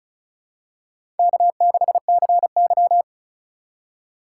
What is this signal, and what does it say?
Morse code, heard as a keyed tone, about 36 wpm, 700 Hz. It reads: K6CY
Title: USA Call Signs - 4 Characters - 4x Character Spacing - 35wpm